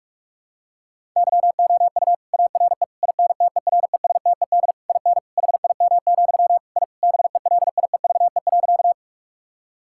Morse code, heard as a keyed tone, about 41 wpm, 700 Hz. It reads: YOU ARE INTERESTED IN HIM, I BELIEVE.
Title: Sentences from Top 1000 Words - Review Part 6 - 45wpm